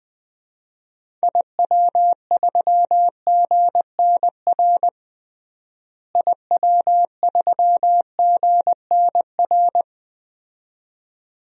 Morse code, heard as a keyed tone, about 21 wpm, 700 Hz. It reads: IW3GNR IW3GNR
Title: DXCC-100 Call Signs - 6 Characters - Repeated 2x - 20wpm